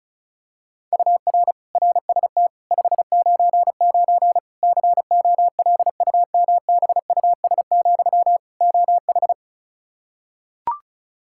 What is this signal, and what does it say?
Morse code, heard as a keyed tone, about 36 wpm, 700 Hz. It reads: UR RST 599 COLUMBUS, OH
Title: K1USN SST QSO Practice 35wpm